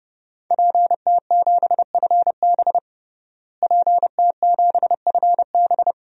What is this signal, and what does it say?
Morse code, heard as a keyed tone, about 32 wpm, 700 Hz. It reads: PT7FB PT7FB
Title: DXCC-100 Call Signs - 5 Characters - Repeated 2x - 30wpm